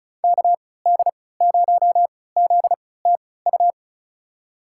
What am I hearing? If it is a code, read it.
Morse code, heard as a keyed tone, about 36 wpm, spaced as for 20 wpm, 700 Hz. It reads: KD0ZTU
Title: USA Call Signs - 3x Character Spacing - 35wpm